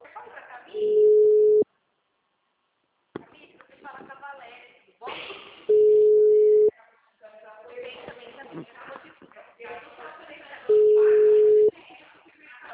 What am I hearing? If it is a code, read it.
Ringback tone (Europe, 425 Hz), with speech.
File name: out- -2011- - - .5242